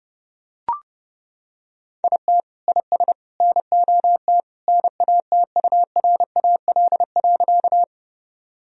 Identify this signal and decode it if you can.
Morse code, heard as a keyed tone, about 30 wpm, 700 Hz. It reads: IT IS NOT NATURAL.
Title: Sentences from Top 1000 Words - Review Part 4 - 30wpm